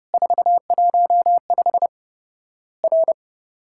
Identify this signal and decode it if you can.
Morse code, heard as a keyed tone, about 30 wpm, 700 Hz. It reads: 415 R
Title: Worked All Europe DX Contest - QTC Exchange - 30wpm